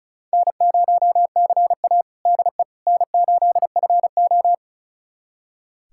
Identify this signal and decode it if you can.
Morse code, heard as a keyed tone, about 36 wpm, 700 Hz. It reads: N0CA DE N8FO
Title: USA Callsign DE Callsign - 4 Characters - 35wpm